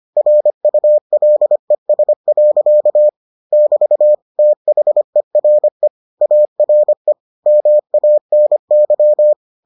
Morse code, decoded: RULES. <BT> THERE ARE MANY

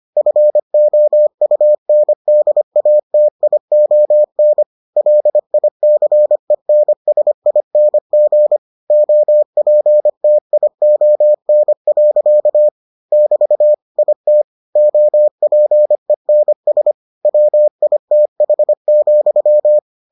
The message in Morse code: FOUNDATION LICENSING OPTION. <BT> IT OPENS WITH,

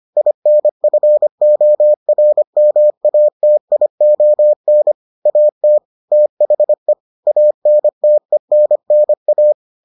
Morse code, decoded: INFORMATION AT THE ANTENNA

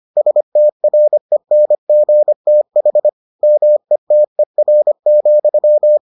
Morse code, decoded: STRENGTH METER,

